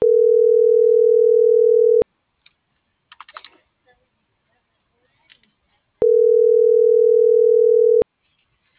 Ringback tone (North America), 440 + 480 Hz.